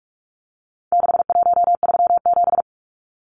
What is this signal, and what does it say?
Morse code, heard as a keyed tone, about 44 wpm, 700 Hz. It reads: 6137
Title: Numbers - 4-Digit 45wpm